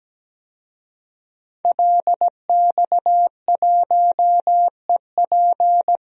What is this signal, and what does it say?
Morse code, heard as a keyed tone, about 17 wpm, 700 Hz. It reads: LX1EP
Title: DXCC-100 Call Signs - 5 Characters - 17wpm